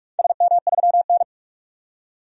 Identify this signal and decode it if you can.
Morse code, heard as a keyed tone, about 45 wpm, 700 Hz. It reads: SM3N
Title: DXCC-100 Call Signs - 4 Characters - 45wpm